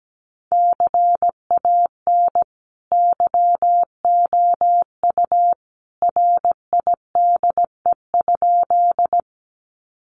Morse code, heard as a keyed tone, about 17 wpm, 700 Hz. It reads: CAN YOU RIDE?
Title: Sentences from Top 1200 Words 17wpm